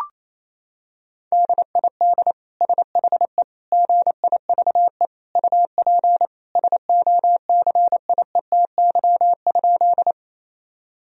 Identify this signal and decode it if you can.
Morse code, heard as a keyed tone, about 28 wpm, 700 Hz. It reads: DID SHE GIVE UP SOCIETY?
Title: Sentences from Top 1000 Words - Review Part 4 - 28wpm